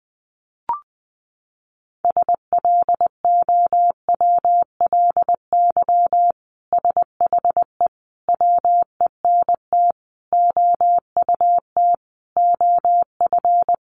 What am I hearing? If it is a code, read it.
Morse code, heard as a keyed tone, about 20 wpm, 700 Hz. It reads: SLOWLY SHE WENT OUT OF
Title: Sentences from Top 1100 Words 20wpm